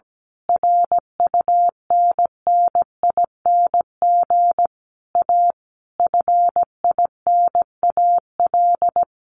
Morse code, heard as a keyed tone, about 17 wpm, 700 Hz. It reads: RUNNING A FINAL
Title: A Sound Of Thunder 17wpm